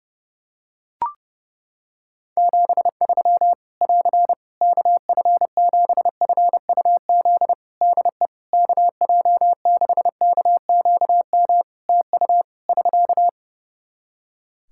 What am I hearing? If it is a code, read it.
Morse code, heard as a keyed tone, about 32 wpm, 700 Hz. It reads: 73 <AR> KF7FUZ DE KJ6KQM TU <SK>
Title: Practice QSO - Exchange 4 - Ending - 30wpm